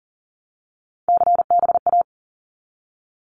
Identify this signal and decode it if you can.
Morse code, heard as a keyed tone, about 38 wpm, 700 Hz. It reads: CBA
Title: Abbreviations 40wpm